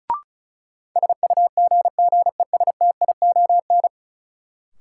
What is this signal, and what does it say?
Morse code, heard as a keyed tone, about 34 wpm, 700 Hz. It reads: SUGGESTION
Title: Sentences from Top 1600 Words - New Word Drill - 35wpm